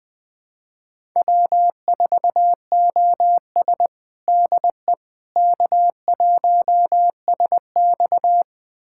Morse code, heard as a keyed tone, about 19 wpm, 700 Hz. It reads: W4OS DE K1SX
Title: USA Callsign DE Callsign - 4 Characters - 20wpm